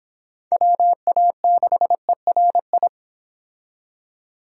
Morse code, heard as a keyed tone, about 29 wpm, 700 Hz. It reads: WA6ERI